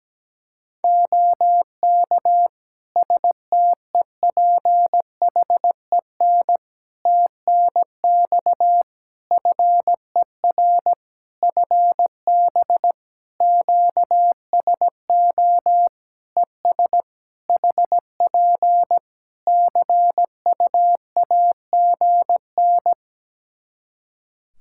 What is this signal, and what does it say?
Morse code, heard as a keyed tone, about 17 wpm, 700 Hz. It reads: OK STEPHEN TNX FER FB QSO ES HP CUAGN
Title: K1USN SST QSO Practice 17wpm